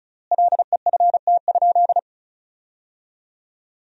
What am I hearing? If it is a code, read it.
Morse code, heard as a keyed tone, about 34 wpm, 700 Hz. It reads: LEFT?
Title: 4x Word-Spacing - Sentences from Top 300 Words - 35wpm